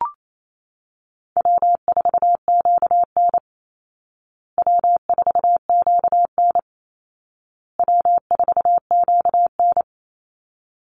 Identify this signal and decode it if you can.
Morse code, heard as a keyed tone, about 28 wpm, 700 Hz. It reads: W4QN W4QN W4QN
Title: Easy Call Signs Repeated 3x - 28wpm